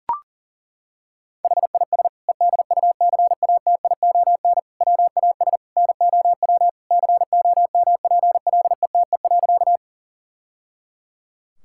Morse code, heard as a keyed tone, about 37 wpm, 700 Hz. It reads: HIS EDUCATION WAS NOW COMPLETE.
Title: Sentences from Top 1000 Words - Review Part 6 - 40wpm